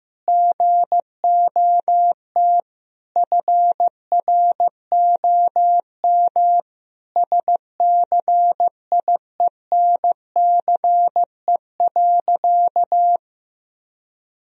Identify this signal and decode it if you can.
Morse code, heard as a keyed tone, about 15 wpm, 700 Hz. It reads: GOT FROM SCIENCE.
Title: Sentences from Top 1000 Words - Review Part 2 - 15wpm